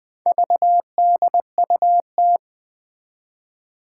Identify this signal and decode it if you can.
Morse code, heard as a keyed tone, about 20 wpm, 700 Hz. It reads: VDUT
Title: Four Challenging Letter-Number Combinations 20wpm